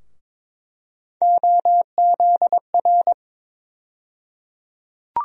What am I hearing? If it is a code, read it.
Morse code, heard as a keyed tone, about 22 wpm, 700 Hz. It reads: OZR
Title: USA Call Sign Postfixes - 3 Letters 22wpm